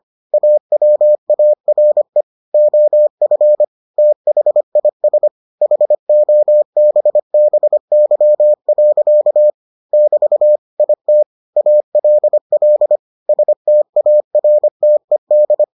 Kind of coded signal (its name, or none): Morse code